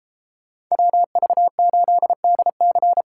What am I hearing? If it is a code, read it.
Morse code, heard as a keyed tone, about 34 wpm, 700 Hz. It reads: WV8DC
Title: Speed Racing - USA 2x2 Call Signs 28wpm